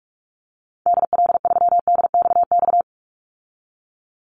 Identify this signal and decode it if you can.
Morse code, heard as a keyed tone, about 44 wpm, 700 Hz. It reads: DL3DXX
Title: Call Signs Repeated 3x - 45wpm